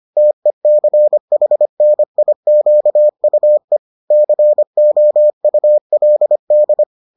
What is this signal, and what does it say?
Morse code, heard as a keyed tone, about 25 wpm, 600 Hz. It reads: TECHNIQUE COULD